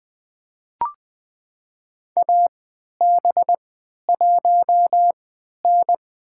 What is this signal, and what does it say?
Morse code, heard as a keyed tone, about 21 wpm, spaced as for 12 wpm, 700 Hz. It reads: AB1N
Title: USA Call Signs - 3x Character Spacing - 20wpm